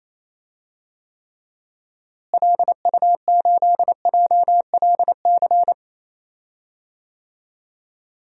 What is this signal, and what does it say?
Morse code, heard as a keyed tone, about 28 wpm, 700 Hz. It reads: LU8JLC
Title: DXCC-100 Call Signs - 6 Characters - Repeated 2x - 28wpm